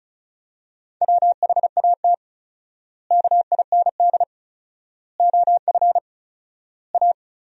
Morse code, heard as a keyed tone, about 36 wpm, 700 Hz. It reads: WHAT KIND OF A
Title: 4x Word-Spacing - Sentences from Top 400 Words - 35wpm